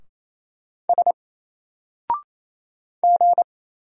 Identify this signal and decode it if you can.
Morse code, heard as a keyed tone, about 28 wpm, 700 Hz. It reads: S G
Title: Single Character (Letter:Number:Punctuation:Prosign) Alt 28wpm